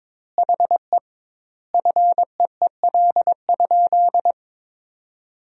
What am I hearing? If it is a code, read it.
Morse code, heard as a keyed tone, about 22 wpm, 700 Hz. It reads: HE FEEL?
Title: 2x Word-Spacing - Sentences from Top 400 Words - 22wpm